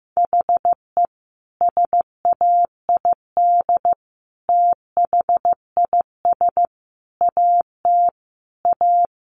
Morse code, heard as a keyed tone, about 15 wpm, 700 Hz. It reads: HE SAID THIS AT A
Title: Sentences from Top 1000 Words 15wpm